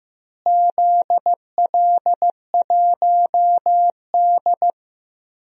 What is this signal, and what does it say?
Morse code, heard as a keyed tone, about 15 wpm, 700 Hz. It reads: ZL1D
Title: DXCC-100 Call Signs - 4 Characters - Repeated 3x - 15wpm